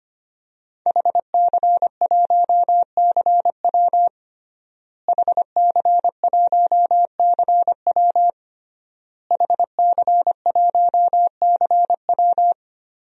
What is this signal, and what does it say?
Morse code, heard as a keyed tone, about 25 wpm, 700 Hz. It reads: HC1CW HC1CW HC1CW
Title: DXCC-100 Call Signs - 5 Characters - Repeated 3x - 25wpm